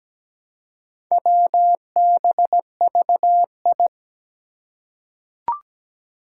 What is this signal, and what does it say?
Morse code, heard as a keyed tone, about 17 wpm, 700 Hz. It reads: WBVI
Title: Four Challenging Letter-Number Combinations 17wpm